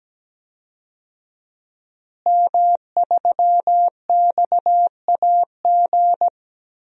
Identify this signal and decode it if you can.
Morse code, heard as a keyed tone, about 17 wpm, 700 Hz. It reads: M3XAG
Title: DXCC-100 Call Signs - 5 Characters - Repeated 3x - 17wpm